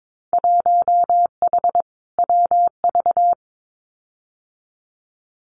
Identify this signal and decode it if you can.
Morse code, heard as a keyed tone, about 22 wpm, 700 Hz. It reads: 1H WV
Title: Winter Field Day Exchanges 22wpm